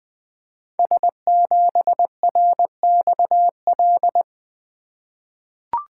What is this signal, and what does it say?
Morse code, heard as a keyed tone, about 20 wpm, 700 Hz. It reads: S7RXL
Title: Code Groups - 5 Letter:Numbers - 20wpm